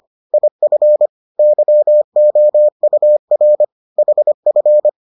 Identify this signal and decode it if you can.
Morse code, heard as a keyed tone, about 25 wpm, 600 Hz. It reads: IF YOUR HF